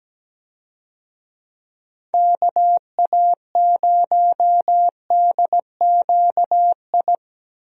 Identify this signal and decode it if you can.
Morse code, heard as a keyed tone, about 17 wpm, 700 Hz. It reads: KA0DQI
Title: USA Call Signs - 6 Characters - Repeated 2x - 17wpm